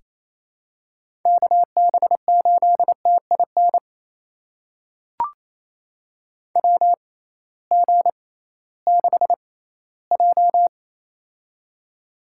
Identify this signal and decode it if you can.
Morse code, heard as a keyed tone, about 28 wpm, spaced as for 16 wpm, 700 Hz. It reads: KB8TIN WG6J
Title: USA Call Signs - 6x Character Spacing - 28wpm